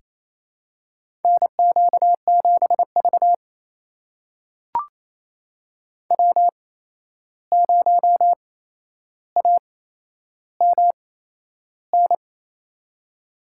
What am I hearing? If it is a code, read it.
Morse code, heard as a keyed tone, about 29 wpm, spaced as for 11 wpm, 700 Hz. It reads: NQ7V W0AMN
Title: USA Call Signs - 8x Character Spacing - 28wpm